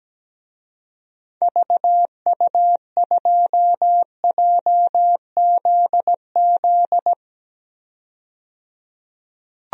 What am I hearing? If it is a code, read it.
Morse code, heard as a keyed tone, about 17 wpm, 700 Hz. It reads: VU2JZZ
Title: DXCC-100 Call Signs - 6 Characters - 17wpm